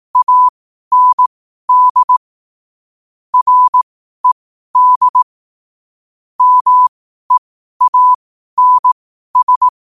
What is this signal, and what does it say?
Morse code, heard as a keyed tone, about 18 wpm, spaced as for 12 wpm, 1000 Hz. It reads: AND RED MEANS
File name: qotdfiles / 12wpm